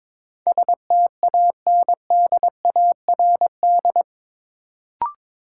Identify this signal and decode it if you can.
Morse code, heard as a keyed tone, about 22 wpm, 700 Hz. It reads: STANDARD
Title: Sentences from Top 1200 Words - New Word Drill Alt - 22wpm